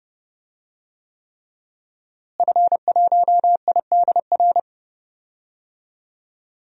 Morse code, heard as a keyed tone, about 31 wpm, 700 Hz. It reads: F1IDR
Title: DXCC-100 Call Signs - 5 Characters - Repeated 3x - 30wpm